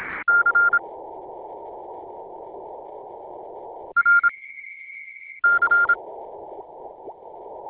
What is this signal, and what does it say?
Morse code, heard as a keyed tone, about 26 wpm, spaced as for 7 wpm, 1400 Hz. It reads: C RC